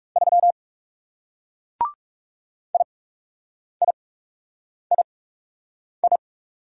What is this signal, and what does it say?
Morse code, heard as a keyed tone, about 38 wpm, spaced as for 8 wpm, 700 Hz. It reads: 3 IIII